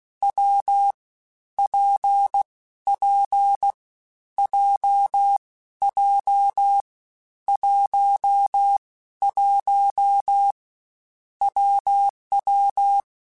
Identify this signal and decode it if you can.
Morse code, heard as a keyed tone, about 16 wpm, 800 Hz. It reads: W P P J J 1 1 WW
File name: Difficult Combos / Difficult Combos-mp3s / WPJ WPJ1 @15